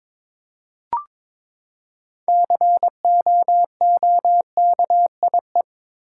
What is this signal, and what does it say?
Morse code, heard as a keyed tone, about 22 wpm, 700 Hz. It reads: COOKIE
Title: Six Letter Words - 22wpm